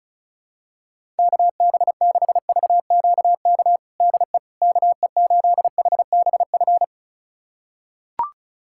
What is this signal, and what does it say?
Morse code, heard as a keyed tone, about 34 wpm, 700 Hz. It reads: KB6VQK DE KE8HBF
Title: USA Callsign DE Callsign - 6 Characters - 35wpm